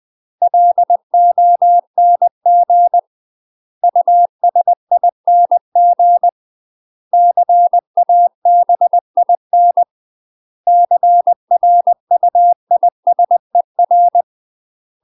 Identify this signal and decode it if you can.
Morse code, heard as a keyed tone, about 20 wpm, 700 Hz. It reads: LONG USING CABIN CRUISER
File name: UK MASTER QSOs TRAINING / 20WPM / UK MASTER QSOs 0073